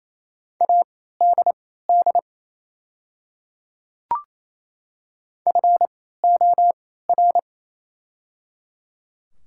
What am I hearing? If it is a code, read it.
Morse code, heard as a keyed tone, about 27 wpm, spaced as for 16 wpm, 700 Hz. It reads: ADD FOR
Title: 3x Character Spacing - Top 300 Words - 28wpm